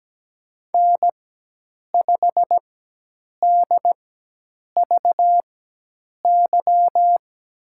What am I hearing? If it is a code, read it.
Morse code, heard as a keyed tone, about 17 wpm, spaced as for 8 wpm, 700 Hz. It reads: N5DVY
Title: USA Call Signs - 4x Character Spacing - 17wpm